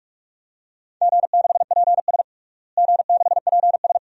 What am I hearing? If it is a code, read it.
Morse code, heard as a keyed tone, about 41 wpm, 700 Hz. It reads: G6PS G6PS
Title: DXCC-100 Call Signs - 4 Characters - Repeated 2x - 45wpm